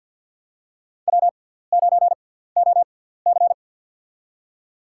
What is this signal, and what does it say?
Morse code, heard as a keyed tone, about 51 wpm, spaced as for 17 wpm, 700 Hz. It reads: W9OC
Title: USA Call Signs - 6x Character Spacing - 50wpm